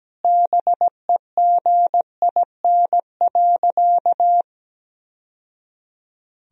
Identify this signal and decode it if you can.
Morse code, heard as a keyed tone, about 17 wpm, 700 Hz. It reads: BEGIN.
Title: Sentences from Top 1000 Words - Review Part 1 - 17wpm